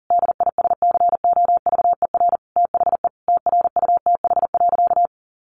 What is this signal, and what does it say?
Morse code, heard as a keyed tone, about 40 wpm, 700 Hz. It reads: DISCOVER THE TRUTH.
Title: Sentences from Top 1100 Words 40wpm